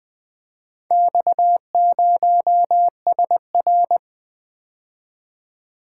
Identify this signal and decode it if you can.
Morse code, heard as a keyed tone, about 20 wpm, 700 Hz. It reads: X0SR
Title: Four Letter-Number Combinations - 20wpm